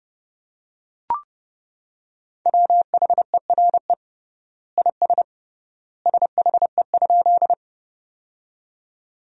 Morse code, heard as a keyed tone, about 29 wpm, 700 Hz. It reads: WHERE IS SHE?
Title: 3x Word-Spacing - Sentences from Top 200 Words - 30wpm